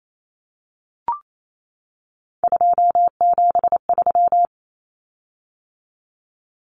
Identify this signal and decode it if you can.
Morse code, heard as a keyed tone, about 28 wpm, 700 Hz. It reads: 273
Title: Common Cut Numbers - 3-Digit 28wpm